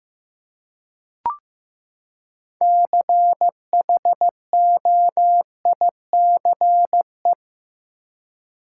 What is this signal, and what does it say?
Morse code, heard as a keyed tone, about 15 wpm, 700 Hz. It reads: CHOICE
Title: Six Letter Words - 15wpm